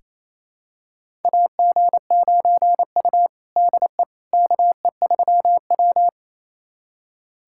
Morse code, heard as a keyed tone, about 28 wpm, 700 Hz. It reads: AG9U DE KE3W
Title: USA Callsign DE Callsign - 4 Characters - 28wpm